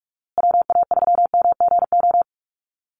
Morse code, heard as a keyed tone, about 44 wpm, 700 Hz. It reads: WA3MGO
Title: USA Call Signs Repeated 3x 45wpm